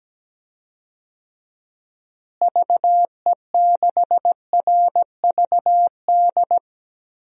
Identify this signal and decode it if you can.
Morse code, heard as a keyed tone, about 17 wpm, 700 Hz. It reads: VE6RVD
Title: DXCC-100 Call Signs - 6 Characters - Repeated 3x - 17wpm